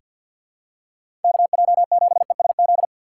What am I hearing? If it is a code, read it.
Morse code, heard as a keyed tone, about 48 wpm, 700 Hz. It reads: KJ7ESZ
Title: USA Call Signs - 6 Characters - 6x Character Spacing - 50wpm